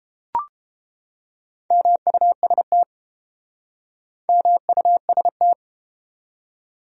Morse code, heard as a keyed tone, about 31 wpm, 700 Hz. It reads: MUST MUST